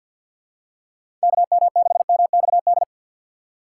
Morse code, heard as a keyed tone, about 48 wpm, 700 Hz. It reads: KM6MXD
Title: USA Call Signs - 6 Characters - 6x Character Spacing - 50wpm